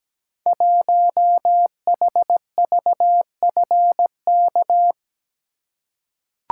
Morse code, heard as a keyed tone, about 17 wpm, 700 Hz. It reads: 1HVFK
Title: Code Groups - 5 Letter:Numbers - 17wpm